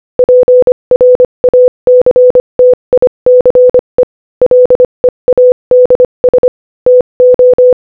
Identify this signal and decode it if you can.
Morse code, heard as a keyed tone, about 25 wpm, 500 Hz. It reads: PRACTICE LEADS TO
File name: Morse-Code-Audio-Speed-25-14